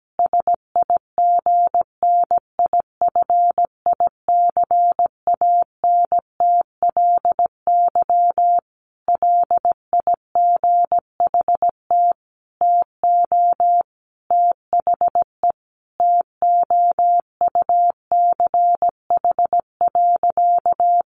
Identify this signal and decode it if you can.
Morse code, heard as a keyed tone, about 17 wpm, 700 Hz. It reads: SIGNIFICANTLY LIGHT TO THE TOUCH.
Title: Sentences from Top 1500 Words 17wpm